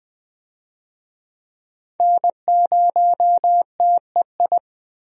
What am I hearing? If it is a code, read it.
Morse code, heard as a keyed tone, about 20 wpm, 700 Hz. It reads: N0TEI
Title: USA Call Signs - 5 Characters - Repeated 3x - 20wpm